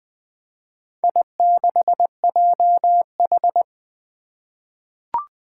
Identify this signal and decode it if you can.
Morse code, heard as a keyed tone, about 20 wpm, 700 Hz. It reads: I6JH
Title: Four Challenging Letter-Number Combinations 20wpm